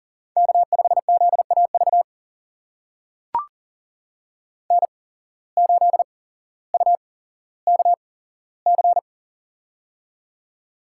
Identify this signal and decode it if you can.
Morse code, heard as a keyed tone, about 40 wpm, spaced as for 17 wpm, 700 Hz. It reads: K5ZAV N8UKC
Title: USA Call Signs - 8x Character Spacing - 40wpm